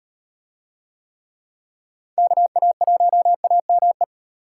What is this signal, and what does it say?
Morse code, heard as a keyed tone, about 38 wpm, 700 Hz. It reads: KA1AME